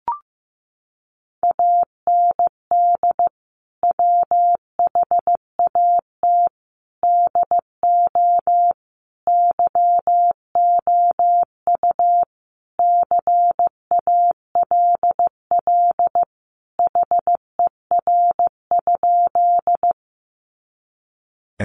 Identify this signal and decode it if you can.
Morse code, heard as a keyed tone, about 15 wpm, 700 Hz. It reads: AND WHAT DO YOU CALL HER?
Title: Sentences from Top 100 Words 15wpm